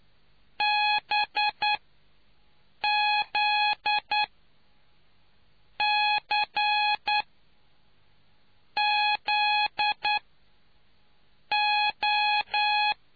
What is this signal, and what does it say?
Morse code, heard as a keyed tone, about 9 wpm, spaced as for 5 wpm, 840 Hz. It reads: BZCZO